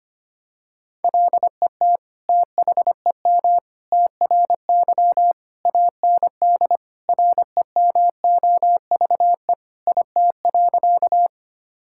Morse code, heard as a keyed tone, about 25 wpm, 700 Hz. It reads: LET THEM TRY AND REMOVE IT.
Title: Sentences from Top 1000 Words - Review Part 5 - 25wpm